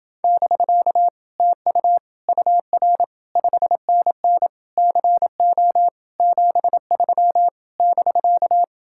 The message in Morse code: <BK> TU UR 5NN CO 73 <BK>

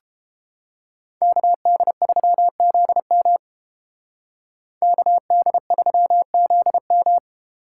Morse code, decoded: KD3ZM KD3ZM

